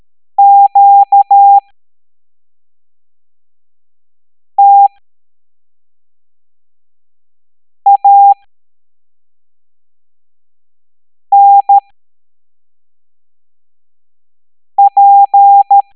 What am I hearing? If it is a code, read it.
Morse code, heard as a keyed tone, about 13 wpm, spaced as for 3 wpm, 800 Hz. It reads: QTANP
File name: hamdata / L07-EX02-HQP